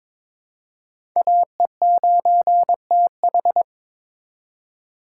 Morse code, heard as a keyed tone, about 22 wpm, 700 Hz. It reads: AE9TH
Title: USA Call Signs - 5 Characters - 2x Character Spacing - 22wpm